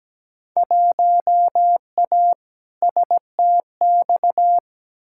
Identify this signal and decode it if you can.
Morse code, heard as a keyed tone, about 17 wpm, 700 Hz. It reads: 1A STX
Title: Field Day Exchanges - 17wpm